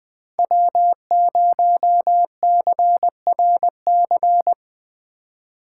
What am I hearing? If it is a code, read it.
Morse code, heard as a keyed tone, about 20 wpm, 700 Hz. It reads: W0CRC